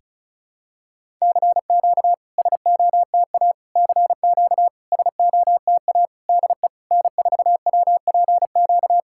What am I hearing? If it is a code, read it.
Morse code, heard as a keyed tone, about 36 wpm, 700 Hz. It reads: CQ SOTA CQ SOTA DE N4WPQ
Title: SOTA QSO Practice 35wpm